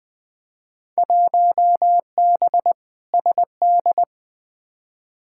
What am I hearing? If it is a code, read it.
Morse code, heard as a keyed tone, about 20 wpm, 700 Hz. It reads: 1B SD
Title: Field Day Exchanges Encore - 20wpm